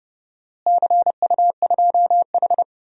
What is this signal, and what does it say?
Morse code, heard as a keyed tone, about 31 wpm, 700 Hz. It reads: CU2H
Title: DXCC-100 Call Signs - 4 Characters - Repeated 3x - 30wpm